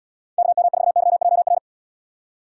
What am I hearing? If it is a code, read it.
Morse code, heard as a keyed tone, about 75 wpm, 700 Hz. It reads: KM4YJR